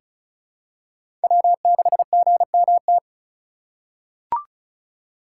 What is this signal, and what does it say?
Morse code, heard as a keyed tone, about 34 wpm, 700 Hz. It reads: W6GMT
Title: USA Call Signs - 5 Characters - 6x Character Spacing - 35wpm